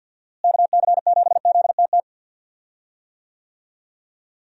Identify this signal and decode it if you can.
Morse code, heard as a keyed tone, about 52 wpm, 700 Hz. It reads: KC7ZTT
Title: USA Call Signs - 6 Characters - Repeated 3x - 50wpm